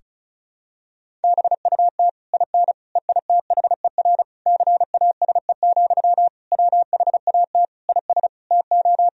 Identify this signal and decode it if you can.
Morse code, heard as a keyed tone, about 36 wpm, 700 Hz. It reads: BUT IN EITHER CASE, WHAT IS TO
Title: Sentences from Top 1000 Words - Review Part 4 - 35wpm